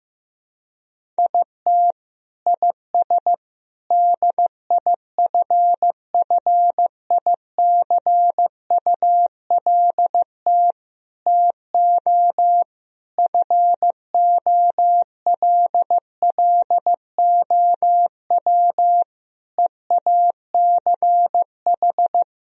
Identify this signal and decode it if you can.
Morse code, heard as a keyed tone, about 15 wpm, 700 Hz. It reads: IT IS DIFFICULT TO FOLLOW EACH
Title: Sentences from Top 1000 Words - Review Part 1 - 15wpm